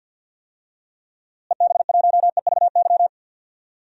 Morse code, heard as a keyed tone, about 52 wpm, 700 Hz. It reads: EB1EVY
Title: DXCC-100 Call Signs - 6 Characters - Repeated 3x - 50wpm